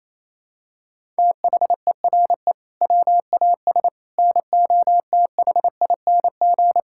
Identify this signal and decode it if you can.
Morse code, heard as a keyed tone, about 28 wpm, 700 Hz. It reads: THERE WAS NOTHING
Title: Sentences from Top 1000 Words - Review Part 5 - 28wpm